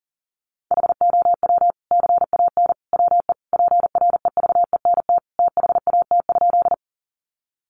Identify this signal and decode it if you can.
Morse code, heard as a keyed tone, about 40 wpm, 700 Hz. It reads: HOW CAN WE PREVENT THAT?
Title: Sentences from Top 1100 Words 40wpm